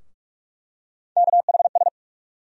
Morse code, heard as a keyed tone, about 43 wpm, 700 Hz. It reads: KHS
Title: USA Call Sign Postfixes - 3 Letters 45wpm